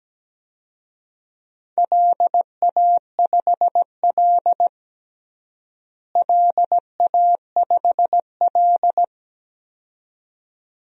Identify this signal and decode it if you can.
Morse code, heard as a keyed tone, about 17 wpm, 700 Hz. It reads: LA5L LA5L
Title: DXCC-100 Call Signs - 4 Characters - Repeated 2x - 17wpm